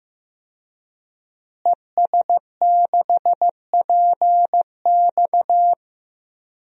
Morse code, heard as a keyed tone, about 15 wpm, 700 Hz. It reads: ES6PX
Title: DXCC-100 Call Signs - 5 Characters - Repeated 3x - 15wpm